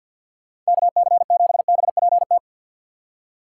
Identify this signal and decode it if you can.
Morse code, heard as a keyed tone, about 62 wpm, 700 Hz. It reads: KC7BPT
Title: Speed Racing - USA Call Signs 50wpm